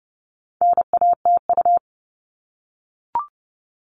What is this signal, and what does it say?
Morse code, heard as a keyed tone, about 30 wpm, 700 Hz. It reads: NATU
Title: All Cut Numbers - 4-Digit 30wpm